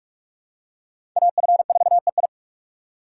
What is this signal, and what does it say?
Morse code, heard as a keyed tone, about 46 wpm, 700 Hz. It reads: AF4EI
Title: USA Call Signs - 5 Characters - 2x Character Spacing - 45wpm